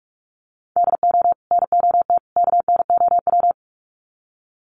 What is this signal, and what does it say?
Morse code, heard as a keyed tone, about 44 wpm, 700 Hz. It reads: DO NOT KNOW
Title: Sets of 3 Words 45wpm